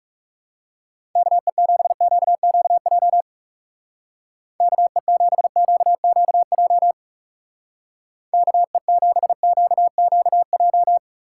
Morse code, heard as a keyed tone, about 38 wpm, 700 Hz. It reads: KE7QQJ KE7QQJ KE7QQJ